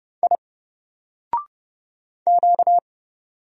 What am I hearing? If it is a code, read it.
Morse code, heard as a keyed tone, about 31 wpm, 700 Hz. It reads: I Q
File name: Single-Letter-Number-30wpm